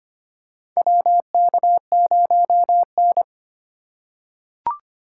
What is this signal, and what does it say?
Morse code, heard as a keyed tone, about 25 wpm, 700 Hz. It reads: WK0N
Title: USA Call Signs - 4 Characters - 3x Character Spacing - 25wpm